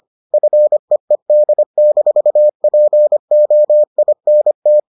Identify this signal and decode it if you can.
Morse code, heard as a keyed tone, about 25 wpm, 600 Hz. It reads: FEED-POINT